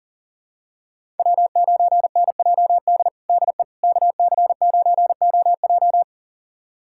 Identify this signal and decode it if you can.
Morse code, heard as a keyed tone, about 40 wpm, 700 Hz. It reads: W9NJD DE KC9OJ
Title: USA Callsign DE Callsign - 5 Characters - 40wpm